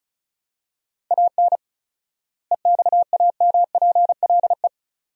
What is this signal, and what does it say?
Morse code, heard as a keyed tone, about 34 wpm, 700 Hz. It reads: AN EXAMPLE
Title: 4x Word-Spacing - Sentences from Top 400 Words - 35wpm